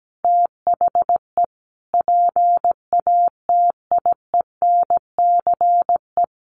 Morse code, heard as a keyed tone, about 17 wpm, 700 Hz. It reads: THE PATIENCE
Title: Sentences from Top 1300 Words 17wpm